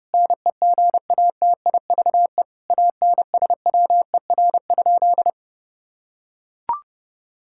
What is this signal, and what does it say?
Morse code, heard as a keyed tone, about 30 wpm, 700 Hz. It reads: NEGATIVE ANSWER?
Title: Sentences from Top 1000 Words - Review Part 6 - 30wpm